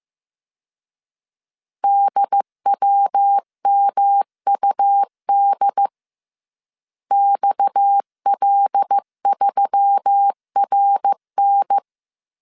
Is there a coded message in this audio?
Yes: Morse code from 1.8 to 11.8 s, heard as a keyed tone, about 15 wpm, 800 Hz. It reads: DWMUD XL3RN